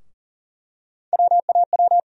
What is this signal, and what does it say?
Morse code, heard as a keyed tone, about 40 wpm, 700 Hz. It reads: WAW
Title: USA Call Sign Postfixes - 3 Letters 40wpm